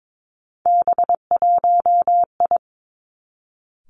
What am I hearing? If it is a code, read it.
Morse code, heard as a keyed tone, about 22 wpm, 700 Hz. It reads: B1I
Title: Three Challenging Letter-Number Combinations 22wpm